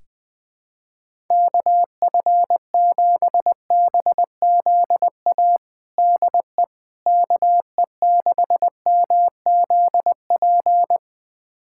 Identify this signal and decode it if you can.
Morse code, heard as a keyed tone, about 21 wpm, 700 Hz. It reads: KF7BZA DE KE6MZP
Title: USA Callsign DE Callsign - 6 Characters - 20wpm